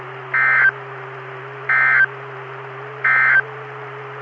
Emergency Alert System (SAME) tones, heard as harsh buzzing bursts. It is heard from 0.2 to 3.5 s.
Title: Saline, AK SKYWARN Repeater System - W5RHS